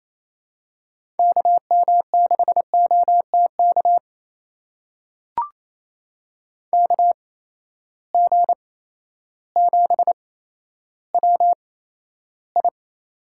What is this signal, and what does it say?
Morse code, heard as a keyed tone, about 28 wpm, spaced as for 13 wpm, 700 Hz. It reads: KM6OTK KG7WI
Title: USA Call Signs - 6 Characters - 8x Character Spacing - 28wpm